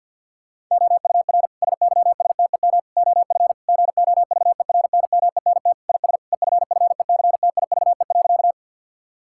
Morse code, heard as a keyed tone, about 48 wpm, 700 Hz. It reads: OUR SYSTEM OF GOVERNMENT IS EFFECTIVE.
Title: Sentences from Top 1000 Words - Review Part 6 - 50wpm